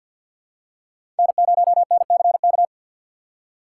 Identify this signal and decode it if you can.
Morse code, heard as a keyed tone, about 47 wpm, 700 Hz. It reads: N0NCK
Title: USA Call Signs - 5 Characters - 4x Character Spacing - 50wpm